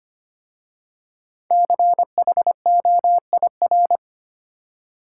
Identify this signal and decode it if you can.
Morse code, heard as a keyed tone, about 25 wpm, 700 Hz. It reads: CHOIR
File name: 5 Letter Words - First 3 Letters Given - 25wpm